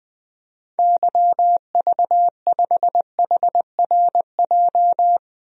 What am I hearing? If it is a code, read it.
Morse code, heard as a keyed tone, about 21 wpm, 700 Hz. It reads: YV5HRJ
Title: DXCC-100 Call Signs - 6 Characters - 20wpm